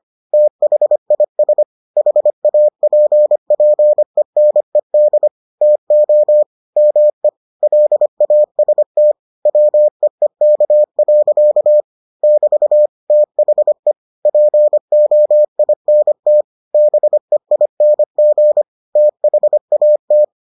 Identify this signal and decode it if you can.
Morse code, heard as a keyed tone, about 25 wpm, 600 Hz. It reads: THIS HAPPENED TO ME LAST WEEK. <BT> THE POINT BEING THAT